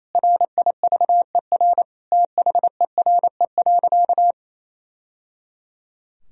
Morse code, heard as a keyed tone, about 28 wpm, 700 Hz. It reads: RIVER THERE.
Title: Sentences from Top 1000 Words - Review Part 3 - 28wpm